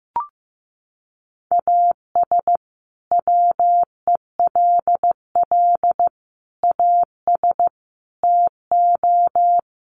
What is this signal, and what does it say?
Morse code, heard as a keyed tone, about 15 wpm, 700 Hz. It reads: AS WELL AS TO
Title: Sets of 5 Words Encore2 15wpm